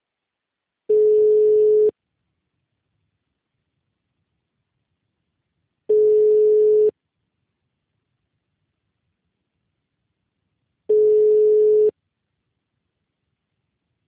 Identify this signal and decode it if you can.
Ringback tone (Europe, 425 Hz).